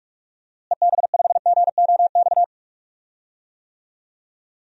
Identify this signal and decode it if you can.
Morse code, heard as a keyed tone, about 46 wpm, 700 Hz. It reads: EB5GOX
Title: DXCC-100 Call Signs - 6 Characters - Repeated 3x - 45wpm